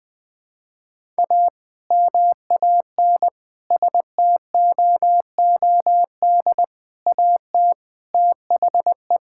Morse code, heard as a keyed tone, about 20 wpm, 700 Hz. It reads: A MAN STOOD AT THE
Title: Sentences with 2-Word Phrases - Spoken Beginning - 20wpm